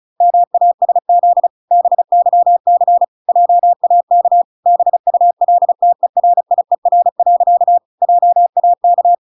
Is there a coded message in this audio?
Yes: Morse code from 0.2 to 9.2 s, heard as a keyed tone, about 34 wpm, 700 Hz. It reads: MASZ BYC JAK BULTERIER. JAK